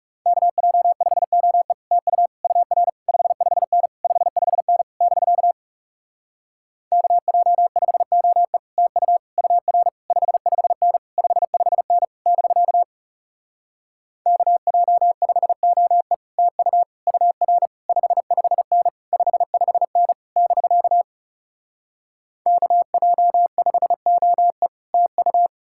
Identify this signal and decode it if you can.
Morse code, heard as a keyed tone, about 40 wpm, 700 Hz. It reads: KJ5OE TU UR 55N 55N <BK> KJ5OE TU UR 55N 55N <BK> KJ5OE TU UR 55N 55N <BK> KJ5OE TU